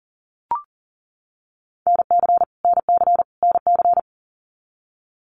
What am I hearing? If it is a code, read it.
Morse code, heard as a keyed tone, about 40 wpm, 700 Hz. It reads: NC NC NC
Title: US States Repeated 3x 40wpm